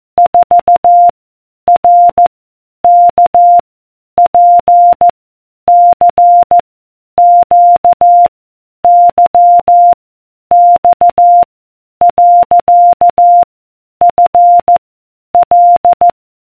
Morse code, heard as a keyed tone, about 14 wpm, 700 Hz. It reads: 4 R K P C Q Y X . F L